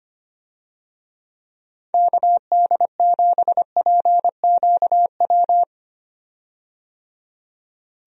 Morse code, heard as a keyed tone, about 25 wpm, 700 Hz. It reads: KD7PQW
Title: USA Call Signs - 6 Characters - 25wpm